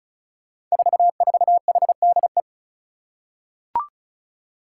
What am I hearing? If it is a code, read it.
Morse code, heard as a keyed tone, about 36 wpm, 700 Hz. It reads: 44HDE
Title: Five Challenging Letter-Number Combinations 35wpm